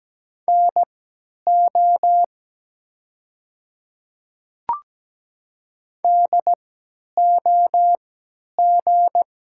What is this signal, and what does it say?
Morse code, heard as a keyed tone, about 17 wpm, spaced as for 10 wpm, 700 Hz. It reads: NO DOG
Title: 3x Character Spacing - Top 500 Words - 17wpm